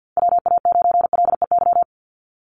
Morse code, heard as a keyed tone, about 50 wpm, 700 Hz. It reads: WA9LEY
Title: Call Signs Repeated 2x - 50wpm